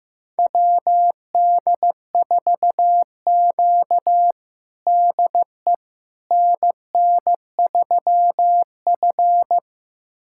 Morse code, heard as a keyed tone, about 16 wpm, 700 Hz. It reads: WD4Q DE NN3F
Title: USA Callsign DE Callsign - 4 Characters - 15wpm